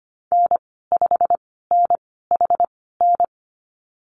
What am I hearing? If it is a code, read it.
Morse code, heard as a keyed tone, about 25 wpm, spaced as for 16 wpm, 700 Hz. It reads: N5NHN
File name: Lesson 15g - Number 5 - Call Signs 25wpm